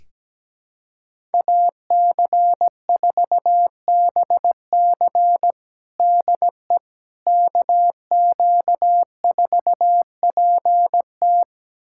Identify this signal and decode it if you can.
Morse code, heard as a keyed tone, about 17 wpm, 700 Hz. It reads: AC4BC DE KQ4PT
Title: USA Callsign DE Callsign - 5 Characters - 17wpm